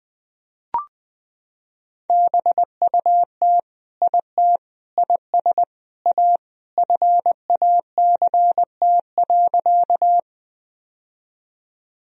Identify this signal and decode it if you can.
Morse code, heard as a keyed tone, about 20 wpm, 700 Hz. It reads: BUT IT IS A FACT.
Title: Sentences from Top 1000 Words - Review Part 3 - 20wpm